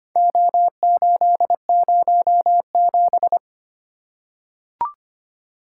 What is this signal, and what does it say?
Morse code, heard as a keyed tone, about 25 wpm, 700 Hz. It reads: O807
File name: practice / 25wpm / Four Letter-Number Combinations - 25wpm